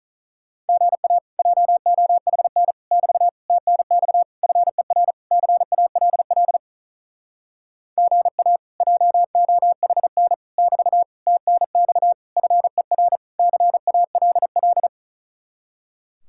Morse code, heard as a keyed tone, about 43 wpm, 700 Hz. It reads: GA JOHN <BT> TNX FER CALL GA JOHN <BT> TNX FER CALL
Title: Speed-Racing Common QSO Phrases Encore - 35wpm